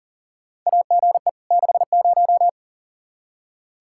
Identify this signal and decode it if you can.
Morse code, heard as a keyed tone, about 40 wpm, 700 Hz. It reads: AGE 60
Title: Practice QSO - Exchange 3 - More Info - 40wpm